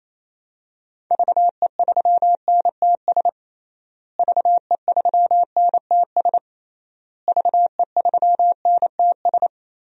Morse code, heard as a keyed tone, about 27 wpm, 700 Hz. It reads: VE3NTS VE3NTS VE3NTS
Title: DXCC-100 Call Signs - 6 Characters - Repeated 3x - 28wpm